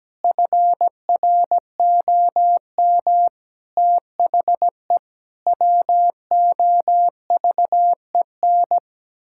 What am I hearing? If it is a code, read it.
Morse code, heard as a keyed tone, about 17 wpm, 700 Hz. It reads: FROM THE WOVEN
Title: Pangram Challenge v2 - 17wpm